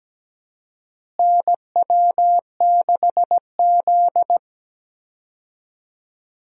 Morse code, heard as a keyed tone, about 17 wpm, 700 Hz. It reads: NW6Z
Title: USA Call Signs - 4 Characters - Repeated 3x - 17wpm